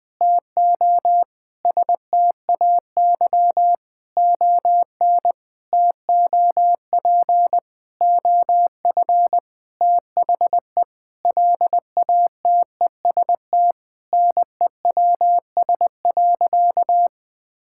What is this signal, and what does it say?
Morse code, heard as a keyed tone, about 21 wpm, 700 Hz. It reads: TO STAY ON TOP OF THE LATEST NEWS.
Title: Sentences with 6-Word Phrases - Spoken Beginning - 20wpm